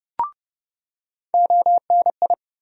Morse code, heard as a keyed tone, about 30 wpm, 700 Hz. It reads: ONI
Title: USA Call Sign Postfixes - 3 Letters 30wpm